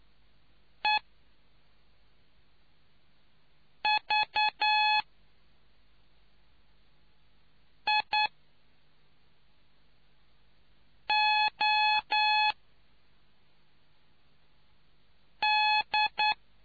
Morse code, heard as a keyed tone, about 9 wpm, spaced as for 3 wpm, 850 Hz. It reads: EVIOD